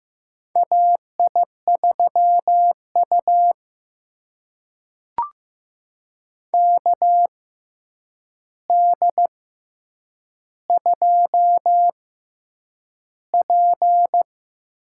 Morse code, heard as a keyed tone, about 15 wpm, spaced as for 8 wpm, 700 Hz. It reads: AI3U KD2P
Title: USA Call Signs - 6x Character Spacing - 15wpm